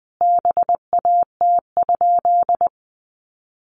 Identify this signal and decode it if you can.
Morse code, heard as a keyed tone, about 20 wpm, 700 Hz. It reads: BAT?
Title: Sentences from Top 1500 Words 20wpm